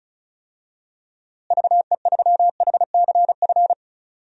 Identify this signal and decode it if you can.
Morse code, heard as a keyed tone, about 36 wpm, 700 Hz. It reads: VE3HCF
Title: DXCC-100 Call Signs - 6 Characters - Repeated 3x - 35wpm